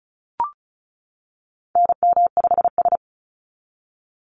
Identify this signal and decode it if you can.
Morse code, heard as a keyed tone, about 36 wpm, 700 Hz. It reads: NM5S
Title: NA SOTA Activator Callsigns 35wpm